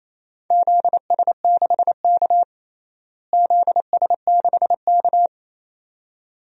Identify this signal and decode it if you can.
Morse code, heard as a keyed tone, about 28 wpm, 700 Hz. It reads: ZS6K ZS6K
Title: DXCC-100 Call Signs - 4 Characters - Repeated 2x - 28wpm